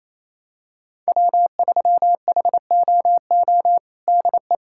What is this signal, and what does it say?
Morse code, heard as a keyed tone, about 28 wpm, 700 Hz. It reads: W3HOO DE
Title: USA Callsign DE Callsign - 5 Characters - 28wpm